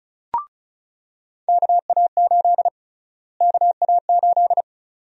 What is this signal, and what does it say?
Morse code, heard as a keyed tone, about 36 wpm, 700 Hz. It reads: KA8 KA8
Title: USA Call Sign Prefixes - Repeated 2x - 35wpm